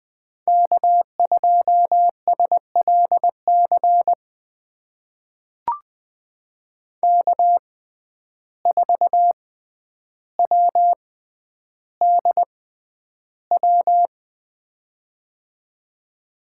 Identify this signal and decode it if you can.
Morse code, heard as a keyed tone, about 20 wpm, spaced as for 10 wpm, 700 Hz. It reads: K2SLC K4WDW
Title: USA Call Signs - 5 Characters - 6x Character Spacing - 20wpm